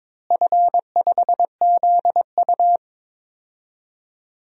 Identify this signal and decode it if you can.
Morse code, heard as a keyed tone, about 22 wpm, 700 Hz. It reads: F5ZU
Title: DXCC-100 Call Signs - 4 Characters - Repeated 3x - 22wpm